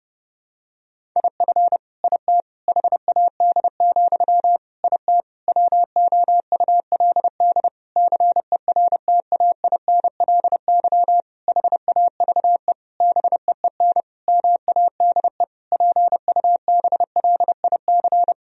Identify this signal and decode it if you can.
Morse code, heard as a keyed tone, about 30 wpm, 700 Hz. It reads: IF IT HAD, IT WOULD CERTAINLY HAVE BEEN MADE PUBLIC
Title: Sentences from Top 1000 Words - Review Part 3 - 30wpm